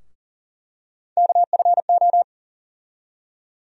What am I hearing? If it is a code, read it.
Morse code, heard as a keyed tone, about 40 wpm, 700 Hz. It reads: KFO
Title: USA Call Sign Postfixes - 3 Letters 40wpm